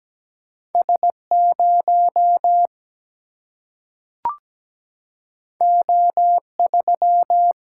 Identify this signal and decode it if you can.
Morse code, heard as a keyed tone, about 17 wpm, 700 Hz. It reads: S0 O3
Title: Single-Letter Single-Number 17wpm